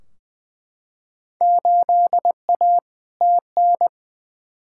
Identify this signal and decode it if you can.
Morse code, heard as a keyed tone, about 20 wpm, 700 Hz. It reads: 8A TN
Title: Field Day Exchanges Encore - 20wpm